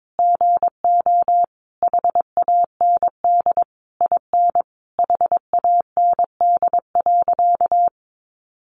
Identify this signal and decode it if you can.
Morse code, heard as a keyed tone, about 22 wpm, 700 Hz. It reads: GO HAND IN HAND.
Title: Sentences from Top 1200 Words 22wpm